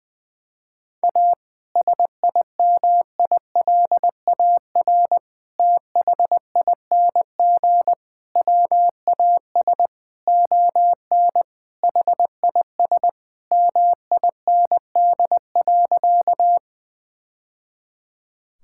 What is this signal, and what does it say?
Morse code, heard as a keyed tone, about 20 wpm, 700 Hz. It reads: A SIMILAR THING WAS ON HIS MIND.
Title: Sentences from Top 1000 Words - Review Part 5 - 20wpm